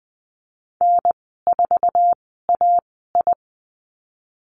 Morse code, heard as a keyed tone, about 20 wpm, spaced as for 14 wpm, 700 Hz. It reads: N4AI
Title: 048 Lesson 9g - Number 4 - Call Signs 20wpm